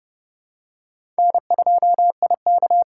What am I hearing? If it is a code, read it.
Morse code, heard as a keyed tone, about 30 wpm, 700 Hz. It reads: N2IK
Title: USA Call Signs - 5 Characters - 8x Character Spacing - 30wpm